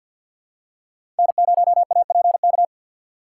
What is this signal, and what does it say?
Morse code, heard as a keyed tone, about 50 wpm, 700 Hz. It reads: N0APK
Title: USA Call Signs - 5 Characters - 2x Character Spacing - 50wpm